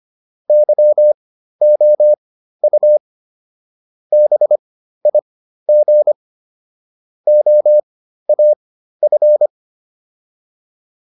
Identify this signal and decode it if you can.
Morse code, heard as a keyed tone, about 25 wpm, spaced as for 13 wpm, 600 Hz. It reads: YOU BIG OAF